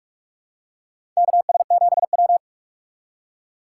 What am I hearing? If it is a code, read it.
Morse code, heard as a keyed tone, about 46 wpm, 690 Hz. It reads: KS7W
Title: USA Call Signs - 4 Characters - 2x Character Spacing - 45wpm